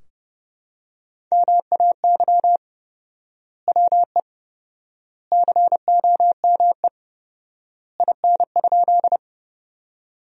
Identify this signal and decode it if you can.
Morse code, heard as a keyed tone, about 30 wpm, 700 Hz. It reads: MAY WE COME IN?
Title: 4x Word-Spacing - Sentences from Top 100 Words - 30wpm